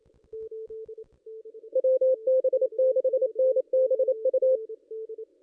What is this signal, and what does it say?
Morse code, heard as a keyed tone, about 27 wpm, 520 Hz. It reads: WB6NBU
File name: contests / ss-cw-k5tr-op-n5rz -1600